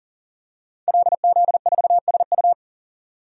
Morse code, heard as a keyed tone, about 40 wpm, 700 Hz. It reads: LZ4SU
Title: DXCC-100 Call Signs - 5 Characters - 40wpm